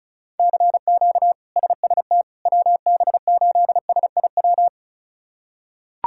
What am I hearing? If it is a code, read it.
Morse code, heard as a keyed tone, about 36 wpm, 700 Hz. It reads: CQ SST WB8SIW
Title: K1USN SST QSO Practice - 35wpm